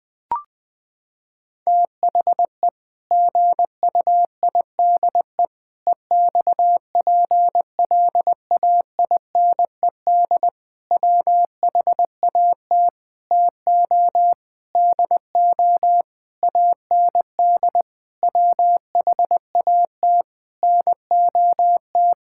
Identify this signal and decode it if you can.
Morse code, heard as a keyed tone, about 20 wpm, 700 Hz. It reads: THE GUIDE EXPLAINED WHAT TO DO AND WHAT NOT
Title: Sentences with 8-Word Phrases - 20wpm